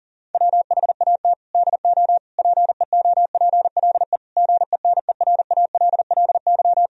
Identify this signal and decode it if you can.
Morse code, heard as a keyed tone, about 40 wpm, 700 Hz. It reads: WHAT DO PEOPLE GENERALLY
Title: Sentences from Top 1000 Words - Review Part 4 - 40wpm